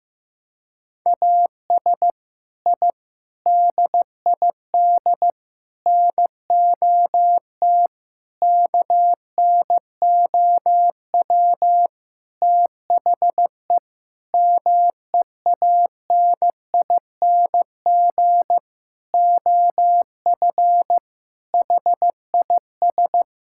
Morse code, heard as a keyed tone, about 15 wpm, 700 Hz. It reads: AS I DID NOT KNOW THE MEANING OF HIS
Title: Sentences from Top 1000 Words - Review Part 6 - 15wpm